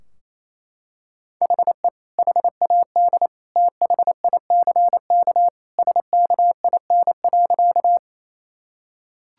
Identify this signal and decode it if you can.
Morse code, heard as a keyed tone, about 28 wpm, 700 Hz. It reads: HE HAD THICK SKIN.
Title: Sentences from Top 1600 Words 28wpm